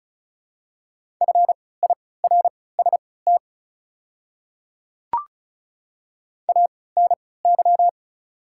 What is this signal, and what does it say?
Morse code, heard as a keyed tone, about 36 wpm, spaced as for 20 wpm, 700 Hz. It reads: FIRST ANY
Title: 3x Character Spacing - Top 200 Words - 35wpm